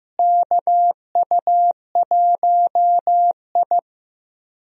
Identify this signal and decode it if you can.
Morse code, heard as a keyed tone, about 15 wpm, 700 Hz. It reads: KU1I
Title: USA Call Signs - 4 Characters - Repeated 2x - 15wpm